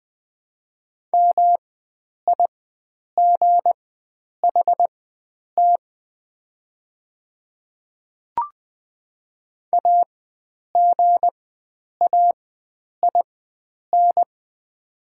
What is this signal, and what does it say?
Morse code, heard as a keyed tone, about 20 wpm, spaced as for 9 wpm, 700 Hz. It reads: MIGHT AGAIN
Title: 4x Character Spacing - Top 500 Words - 20wpm